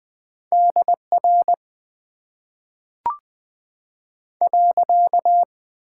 Morse code, heard as a keyed tone, about 20 wpm, spaced as for 5 wpm, 700 Hz. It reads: DR.
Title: Top 90 QSO Elements - 6x Character Spacing - 20wpm